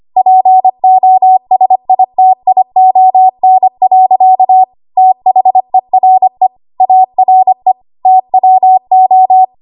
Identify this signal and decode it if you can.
Morse code, heard as a keyed tone, about 25 wpm, 750 Hz. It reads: POSITION. THERE ARE TWO
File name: Archive / 25 WPM / 25WPM